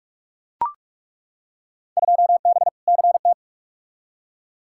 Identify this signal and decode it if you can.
Morse code, heard as a keyed tone, about 42 wpm, 700 Hz. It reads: 2B CT
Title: Field Day Exchanges Encore - 45wpm